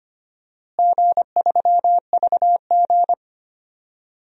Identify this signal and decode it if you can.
Morse code, heard as a keyed tone, about 25 wpm, 700 Hz. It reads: G3VG
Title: DXCC-100 Call Signs - 4 Characters - Repeated 3x - 25wpm